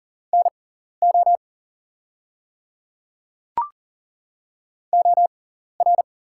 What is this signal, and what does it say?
Morse code, heard as a keyed tone, about 40 wpm, spaced as for 14 wpm, 700 Hz. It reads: NO OR
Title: 6x Character Spacing - Top 200 Words - 40wpm